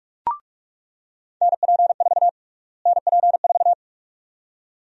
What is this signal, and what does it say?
Morse code, heard as a keyed tone, about 46 wpm, 700 Hz. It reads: NP4 NP4
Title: USA Call Sign Prefixes - Repeated 2x - 45wpm